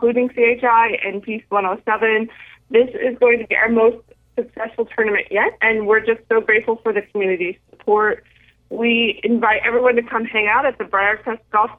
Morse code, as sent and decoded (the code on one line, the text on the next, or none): none